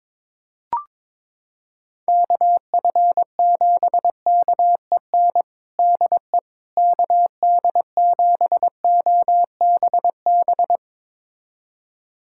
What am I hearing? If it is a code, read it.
Morse code, heard as a keyed tone, about 22 wpm, 700 Hz. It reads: KF7KEN DE KD7OBB
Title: USA Callsign DE Callsign - 6 Characters - 22wpm